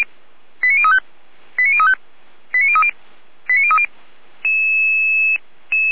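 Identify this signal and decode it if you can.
Five-tone selective call (ZVEI), heard as quick runs of five tones, tones of about 70 ms. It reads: 89035 89035 89030 89030
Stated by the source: ZVEI five-tone selective call 89115 received with an SDR.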